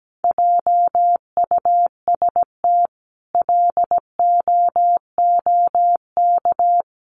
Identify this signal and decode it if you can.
Morse code, heard as a keyed tone, about 17 wpm, 700 Hz. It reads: JUST LOOK
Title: Sentences from Top 1300 Words 17wpm